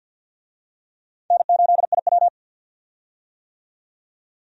Morse code, heard as a keyed tone, about 51 wpm, 700 Hz. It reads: N8IW
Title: USA Call Signs - 4 Characters - 50wpm